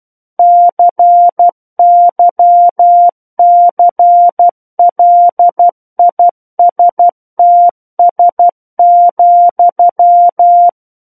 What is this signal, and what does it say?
Morse code, heard as a keyed tone, about 12 wpm, 700 Hz. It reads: CYCLISTS,
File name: Book One / 03-Chapter 3 - 12 WPM